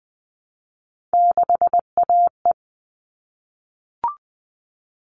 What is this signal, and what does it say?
Morse code, heard as a keyed tone, about 20 wpm, 700 Hz. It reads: 6AE
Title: All Cut Numbers - 3-Digit 20wpm